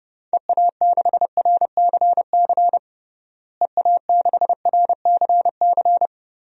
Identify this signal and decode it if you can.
Morse code, heard as a keyed tone, about 30 wpm, 700 Hz. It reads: EA6RCC EA6RCC
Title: DXCC-100 Call Signs - 6 Characters - Repeated 3x - 30wpm